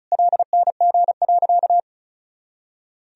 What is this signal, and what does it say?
Morse code, heard as a keyed tone, about 36 wpm, 700 Hz. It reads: LNG.
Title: Sentences from Top 100 Words and Abbreviations 35wpm